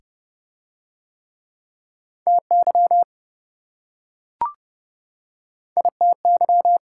Morse code, heard as a keyed tone, about 26 wpm, 700 Hz. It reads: TY ITY